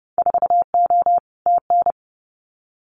Morse code, heard as a keyed tone, about 30 wpm, 700 Hz. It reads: 4O TN
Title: Winter Field Day Exchanges Encore 30wpm